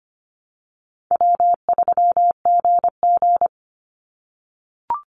Morse code, heard as a keyed tone, about 25 wpm, 700 Hz. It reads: W3GG
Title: Easy Call Signs - 25wpm